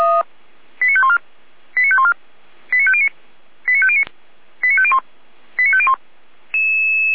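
Five-tone selective call (ZVEI), heard as quick runs of five tones, tones of about 70 ms. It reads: 89514 89514 89509 89509 89501 89501
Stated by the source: ZVEI five-tone call received on SDR.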